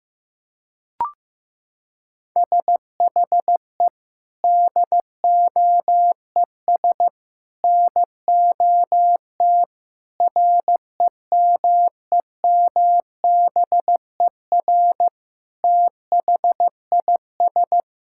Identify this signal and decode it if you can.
Morse code, heard as a keyed tone, about 15 wpm, 700 Hz. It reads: SHE DOES NOT REMEMBER THIS
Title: Sentences from Top 1000 Words - Review Part 5 - 15wpm